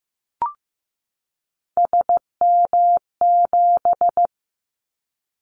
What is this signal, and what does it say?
Morse code, heard as a keyed tone, about 15 wpm, 700 Hz. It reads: SM7
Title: Callsign Prefixes for Top 25 DXCC - 15wpm